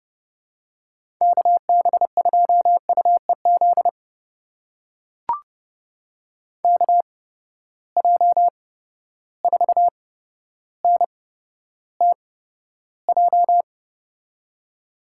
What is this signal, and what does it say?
Morse code, heard as a keyed tone, about 31 wpm, spaced as for 13 wpm, 700 Hz. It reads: KB2UEZ KJ4NTJ
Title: USA Call Signs - 6 Characters - 8x Character Spacing - 30wpm